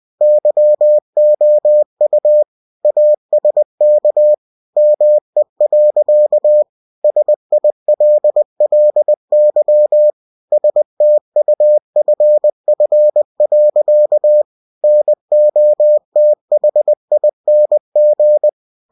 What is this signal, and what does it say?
Morse code, heard as a keyed tone, about 20 wpm, 600 Hz. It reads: YOU ASK ME. SILLY STUFF. NOTHING
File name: cw-books / winnie-the-pooh / winnie-20wpm-page-0109